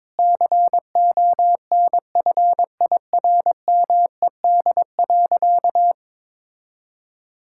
Morse code, heard as a keyed tone, about 22 wpm, 700 Hz. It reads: CONFIRMED.
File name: practice / 22wpm / Sentences with 8-Word Phrases - 22wpm